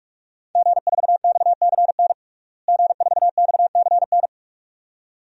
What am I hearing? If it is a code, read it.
Morse code, heard as a keyed tone, about 46 wpm, 700 Hz. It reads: G4XCN G4XCN
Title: DXCC-100 Call Signs - 5 Characters - Repeated 2x - 45wpm